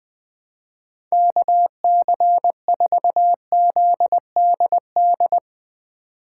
Morse code, heard as a keyed tone, about 20 wpm, 700 Hz. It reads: KC4ZDD
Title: USA Call Signs - 6 Characters - 2x Character Spacing - 20wpm